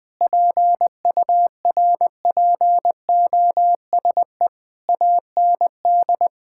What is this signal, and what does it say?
Morse code, heard as a keyed tone, about 20 wpm, 700 Hz. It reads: PURPOSE AND
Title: Sentences with 9-Word Phrases - 20wpm